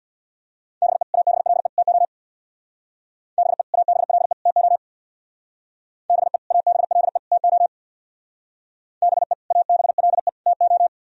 Morse code, heard as a keyed tone, about 61 wpm, 700 Hz. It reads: BE ABLE TO BE ABLE TO BE ABLE TO BE ABLE TO
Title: Speed-Racing - Sets of 3 Words 50wpm